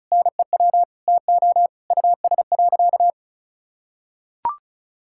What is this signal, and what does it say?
Morse code, heard as a keyed tone, about 34 wpm, 700 Hz. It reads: NEW TO US.
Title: Sentences from Top 1600 Words 35wpm